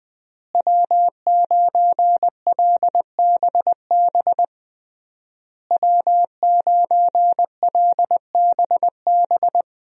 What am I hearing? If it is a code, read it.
Morse code, heard as a keyed tone, about 20 wpm, 700 Hz. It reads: W9LBB W9LBB
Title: USA Call Signs - 5 Characters - Repeated 3x - 20wpm